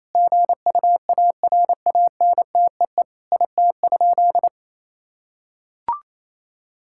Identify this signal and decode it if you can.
Morse code, heard as a keyed tone, about 28 wpm, 700 Hz. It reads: GUARANTEE IT?
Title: Sentences from Top 1600 Words 28wpm